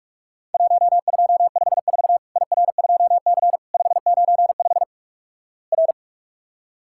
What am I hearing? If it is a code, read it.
Morse code, heard as a keyed tone, about 43 wpm, 700 Hz. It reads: 1254 IR2C 595 R
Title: Worked All Europe DX Contest - QTC Exchange - 45wpm